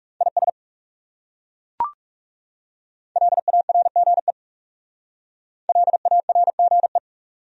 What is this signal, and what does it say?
Morse code, heard as a keyed tone, about 37 wpm, 700 Hz. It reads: IS LARGE LARGE
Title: Speed-Racing Top 200 Words 30wpm